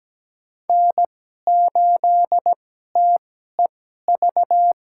Morse code, heard as a keyed tone, about 17 wpm, spaced as for 12 wpm, 700 Hz. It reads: N8TEV
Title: USA Call Signs - 5 Characters - 2x Character Spacing - 17wpm